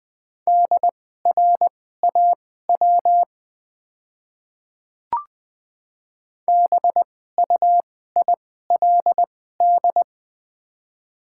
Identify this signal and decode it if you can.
Morse code, heard as a keyed tone, about 21 wpm, spaced as for 15 wpm, 700 Hz. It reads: DRAW BUILD
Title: 2x Character Spacing - Top 300 Words - 20wpm